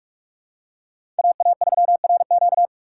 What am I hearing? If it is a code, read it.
Morse code, heard as a keyed tone, about 46 wpm, 700 Hz. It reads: AA3RQ
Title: USA Call Signs - 5 Characters - 4x Character Spacing - 45wpm